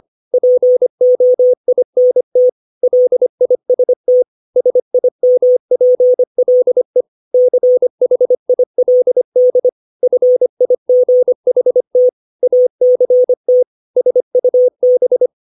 Morse code, heard as a keyed tone, about 25 wpm, 500 Hz. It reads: POINT LIST SIMPLE CHILD FIGHT ACT SUB
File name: 632 words / 632 Words 25WPM